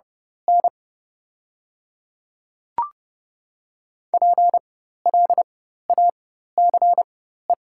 Morse code, heard as a keyed tone, about 30 wpm, spaced as for 14 wpm, 700 Hz. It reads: N PLACE
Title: 4x Character Spacing - Top 300 Words - 30wpm